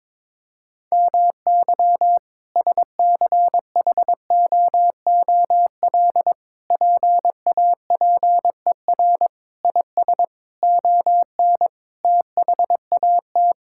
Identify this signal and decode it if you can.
Morse code, heard as a keyed tone, about 22 wpm, 700 Hz. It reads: MY SCHOOL PAPER IS ON THAT
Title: Sentences from Top 1000 Words - Review Part 6 - 22wpm